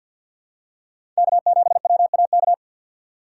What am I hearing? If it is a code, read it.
Morse code, heard as a keyed tone, about 52 wpm, 700 Hz. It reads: K7WAK
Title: USA Call Signs - 5 Characters - 6x Character Spacing - 50wpm